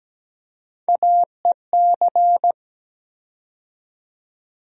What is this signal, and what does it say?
Morse code, heard as a keyed tone, about 17 wpm, 700 Hz. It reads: AEC
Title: USA Call Sign Postfixes - 3 Letters 17wpm